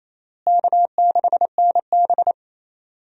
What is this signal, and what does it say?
Morse code, heard as a keyed tone, about 28 wpm, 700 Hz. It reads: K6NB
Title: USA Call Signs - 4 Characters - Repeated 2x - 28wpm